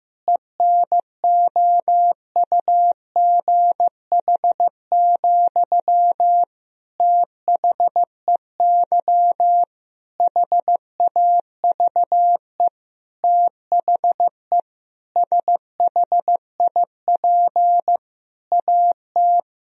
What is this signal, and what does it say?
Morse code, heard as a keyed tone, about 15 wpm, 700 Hz. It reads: ENOUGH, THEY HAVE THE SHIP AT
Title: Sentences from Top 1000 Words - Review Part 2 - 15wpm